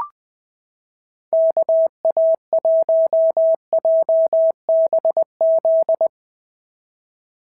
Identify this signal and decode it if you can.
Morse code, heard as a keyed tone, about 20 wpm, 690 Hz. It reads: KA1JBZ
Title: SOTA QSO Practice 20wpm